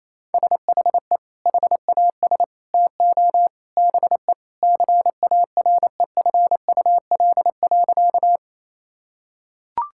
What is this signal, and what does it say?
Morse code, heard as a keyed tone, about 28 wpm, 700 Hz. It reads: SHE HAS TO BE CAREFUL.
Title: Sentences from Top 1600 Words 28wpm